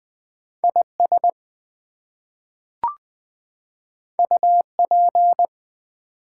Morse code, heard as a keyed tone, about 20 wpm, 700 Hz. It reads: IS UP
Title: Two Letter Words - 20wpm